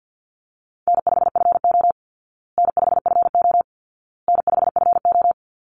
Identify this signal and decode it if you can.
Morse code, heard as a keyed tone, about 48 wpm, 700 Hz. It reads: N5FO N5FO N5FO
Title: Easy Call Signs Repeated 3x - 50wpm